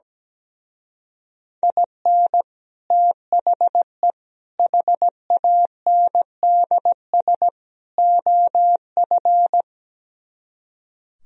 Morse code, heard as a keyed tone, about 17 wpm, 700 Hz. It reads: IN THE HANDS OF
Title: Speed-Racing - Sets of 4 Words 17wpm